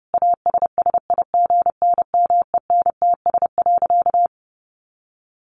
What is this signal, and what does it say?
Morse code, heard as a keyed tone, about 30 wpm, 700 Hz. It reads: ASSIGNMENTS.
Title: Easy Phrases 30wpm